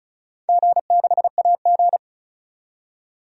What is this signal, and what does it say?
Morse code, heard as a keyed tone, about 36 wpm, 700 Hz. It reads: G6AG
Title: DXCC-100 Call Signs - 4 Characters - 35wpm